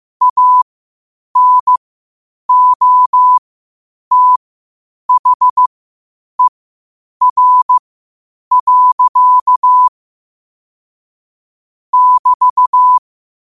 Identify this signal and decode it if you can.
Morse code, heard as a keyed tone, about 15 wpm, spaced as for 8 wpm, 1000 Hz. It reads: ANOTHER. <BT>
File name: qotdfiles / 8wpm